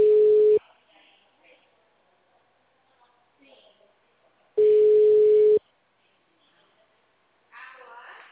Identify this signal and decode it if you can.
Ringback tone (Europe, 425 Hz), with speech.
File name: out- -2044- - - .28754